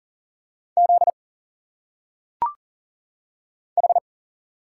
Morse code, heard as a keyed tone, about 40 wpm, 700 Hz. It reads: Z H
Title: Single Letters - Phonetic Pronunciation 40wpm